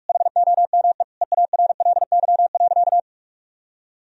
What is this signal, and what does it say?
Morse code, heard as a keyed tone, about 46 wpm, 700 Hz. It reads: HOME EARLY.
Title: Sentences from Top 1000 Words - Review Part 4 - 45wpm